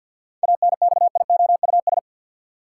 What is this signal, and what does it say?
Morse code, heard as a keyed tone, about 49 wpm, 700 Hz. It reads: ANXIOUS
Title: Seven Letter Words - 50wpm